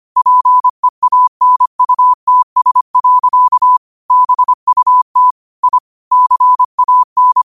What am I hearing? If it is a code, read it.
Morse code, heard as a keyed tone, about 25 wpm, 1000 Hz. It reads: PEANUTS. BUT I CAN
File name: qotdfiles / 25wpm